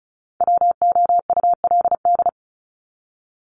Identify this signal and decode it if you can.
Morse code, heard as a keyed tone, about 34 wpm, 700 Hz. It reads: WOULD
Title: Top 200 Words 35wpm